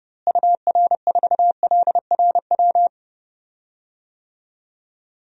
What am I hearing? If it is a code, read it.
Morse code, heard as a keyed tone, about 30 wpm, 700 Hz. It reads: UR4LRW
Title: DXCC-100 Call Signs - 6 Characters - Repeated 3x - 30wpm